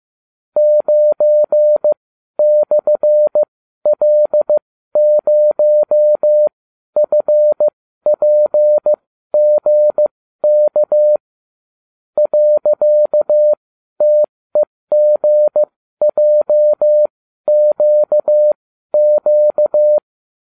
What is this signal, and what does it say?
Morse code, heard as a keyed tone, about 15 wpm, spaced as for 12 wpm, 600 Hz. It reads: 9/L0FPGK .TEGJQQ